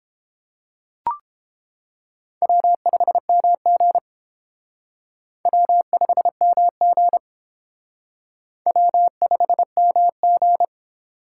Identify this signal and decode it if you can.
Morse code, heard as a keyed tone, about 28 wpm, 700 Hz. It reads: W5MG W5MG W5MG